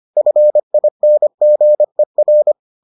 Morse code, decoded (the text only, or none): FINGER